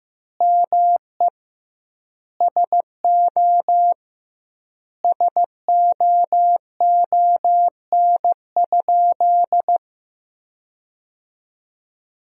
Morse code, heard as a keyed tone, about 15 wpm, 700 Hz. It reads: ME SO SOON?
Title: 2x Word-Spacing - Sentences from Top 400 Words - 15wpm